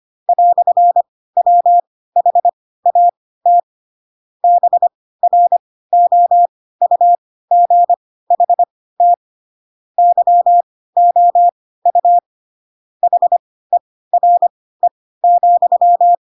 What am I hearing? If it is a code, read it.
Morse code, heard as a keyed tone, about 25 wpm, spaced as for 16 wpm, 700 Hz. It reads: "WHAT BROUGHT YOU HERE,
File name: Pinocchio / Pinocchio ch2 10 25 ik1hge